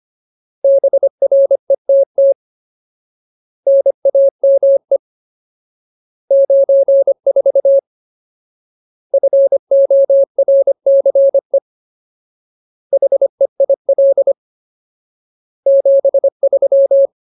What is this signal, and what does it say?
Morse code, heard as a keyed tone, about 25 wpm, 550 Hz. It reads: BRETT NAME 94 FORCE HEIL 73
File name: Ultimate Word List / 25WPM Ultimate